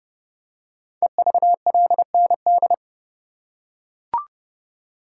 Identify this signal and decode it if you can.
Morse code, heard as a keyed tone, about 28 wpm, 700 Hz. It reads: EVLND
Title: Code Groups - 5 Letter:Numbers - 30wpm